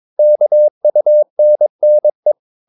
Morse code, heard as a keyed tone, about 22 wpm, 600 Hz. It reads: KUNNE
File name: Ferie QRS 22wpm